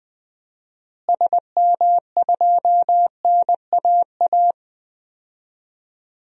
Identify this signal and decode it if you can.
Morse code, heard as a keyed tone, about 20 wpm, 700 Hz. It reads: SM2NAA
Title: DXCC-100 Call Signs - 6 Characters - Repeated 3x - 20wpm